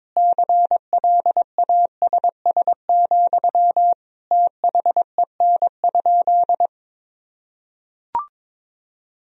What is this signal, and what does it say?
Morse code, heard as a keyed tone, about 22 wpm, 700 Hz. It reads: CLASS, THEN?
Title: Sentences from Top 1000 Words - Review Part 4 - 22wpm